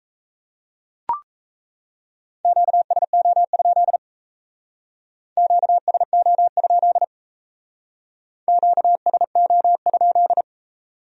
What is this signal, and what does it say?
Morse code, heard as a keyed tone, about 35 wpm, 700 Hz. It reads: QSO? QSO? QSO?